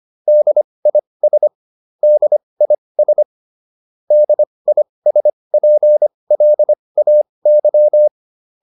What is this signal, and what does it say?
Morse code, heard as a keyed tone, about 25 wpm, spaced as for 20 wpm, 600 Hz. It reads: DIS DIS DISPLAY